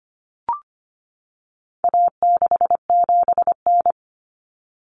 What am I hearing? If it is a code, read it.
Morse code, heard as a keyed tone, about 25 wpm, 700 Hz. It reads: A67N
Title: Common Cut Numbers - 4-Digit 25wpm